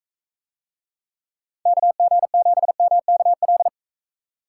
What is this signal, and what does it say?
Morse code, heard as a keyed tone, about 44 wpm, 700 Hz. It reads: KG7MKL